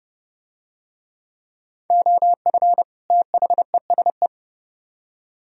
Morse code, heard as a keyed tone, about 29 wpm, 700 Hz. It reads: OF THESE